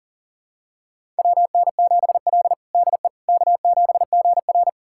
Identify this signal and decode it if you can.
Morse code, heard as a keyed tone, about 40 wpm, 700 Hz. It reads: WN7L DE K7GR
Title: USA Callsign DE Callsign - 4 Characters - 40wpm